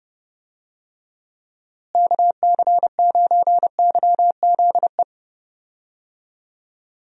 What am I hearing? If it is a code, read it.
Morse code, heard as a keyed tone, about 30 wpm, 700 Hz. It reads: KC9YZE
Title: USA Call Signs - 6 Characters - Repeated 3x - 30wpm